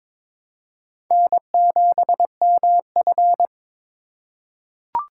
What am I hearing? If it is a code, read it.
Morse code, heard as a keyed tone, about 23 wpm, 700 Hz. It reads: N7MF
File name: practice / 22wpm / Four Letter-Number Combinations - 22wpm